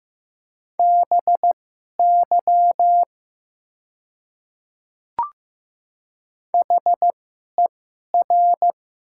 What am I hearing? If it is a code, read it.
Morse code, heard as a keyed tone, about 15 wpm, spaced as for 11 wpm, 700 Hz. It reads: BY HER
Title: 2x Character Spacing - Top 100 Words - 15wpm